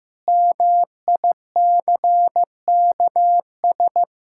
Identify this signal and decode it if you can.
Morse code, heard as a keyed tone, about 15 wpm, 700 Hz. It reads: MICKS
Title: Pangram Challenge v2 - 15wpm